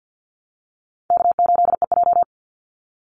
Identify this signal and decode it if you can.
Morse code, heard as a keyed tone, about 50 wpm, 700 Hz. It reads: K8EJ
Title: Call Signs Repeated 2x - 50wpm